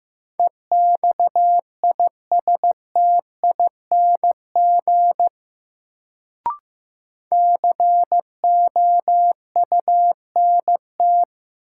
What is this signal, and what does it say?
Morse code, heard as a keyed tone, about 15 wpm, 700 Hz. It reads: EXISTING COUNT
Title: Sentences from Top 1500 Words - New Word Drill Alt - 15wpm